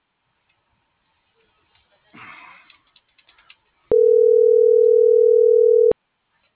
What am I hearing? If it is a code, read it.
Ringback tone (North America), 440 + 480 Hz.